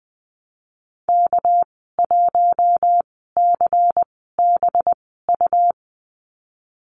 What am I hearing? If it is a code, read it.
Morse code, heard as a keyed tone, about 20 wpm, spaced as for 15 wpm, 700 Hz. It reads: K1CBU
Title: 207 Lesson 33g - Letter j - Call Signs 20wpm